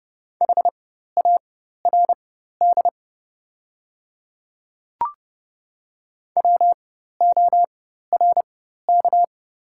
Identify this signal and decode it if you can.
Morse code, heard as a keyed tone, about 29 wpm, spaced as for 14 wpm, 700 Hz. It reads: HARD WORK
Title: 4x Character Spacing - Top 300 Words - 30wpm